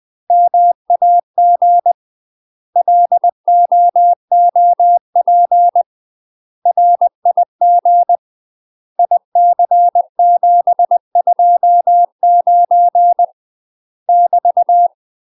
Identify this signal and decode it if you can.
Morse code, heard as a keyed tone, about 20 wpm, 700 Hz. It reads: MAG LOOP RIG IC729 <BT>
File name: UK MASTER QSOs TRAINING / 20WPM / UK MASTER QSOs 0063